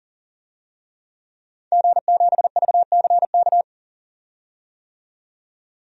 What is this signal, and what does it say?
Morse code, heard as a keyed tone, about 40 wpm, 700 Hz. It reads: G7VCK
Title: DXCC-100 Call Signs - 5 Characters - 40wpm